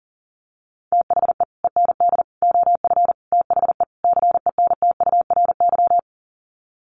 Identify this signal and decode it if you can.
Morse code, heard as a keyed tone, about 40 wpm, 700 Hz. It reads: THE END OF THE CENTURY
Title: Sets of 5 Words Encore2 40wpm